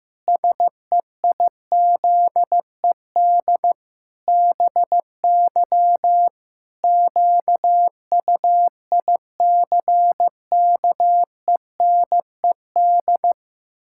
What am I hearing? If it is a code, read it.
Morse code, heard as a keyed tone, about 15 wpm, 700 Hz. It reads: SEIZED BY QUICKENED
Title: Pangram Challenge v2 - 15wpm